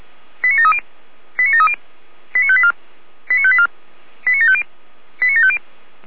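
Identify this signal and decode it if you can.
Five-tone selective call (ZVEI), heard as quick runs of five tones, tones of about 70 ms. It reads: 89730 89730 89584 89584 89750 89750
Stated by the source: ZVEI five-tone selective call received on an SDR.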